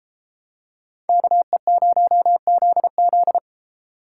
Morse code, heard as a keyed tone, about 30 wpm, 700 Hz. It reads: KE0ZZ